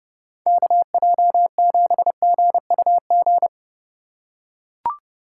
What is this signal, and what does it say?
Morse code, heard as a keyed tone, about 31 wpm, 700 Hz. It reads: KJ7GUG
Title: USA Call Signs - 6 Characters - 4x Character Spacing - 30wpm